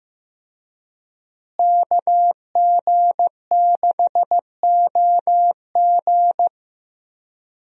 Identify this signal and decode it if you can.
Morse code, heard as a keyed tone, about 15 wpm, 700 Hz. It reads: KG6OG
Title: USA Call Signs - 5 Characters - Repeated 2x - 15wpm